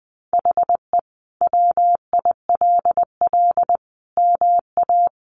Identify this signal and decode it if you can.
Morse code, heard as a keyed tone, about 20 wpm, 700 Hz. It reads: HE WILL MA
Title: Sentences from Top 1200 Words 20wpm